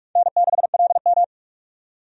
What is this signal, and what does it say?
Morse code, heard as a keyed tone, about 42 wpm, 700 Hz. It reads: N6LM
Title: USA Call Signs - 4 Characters - 45wpm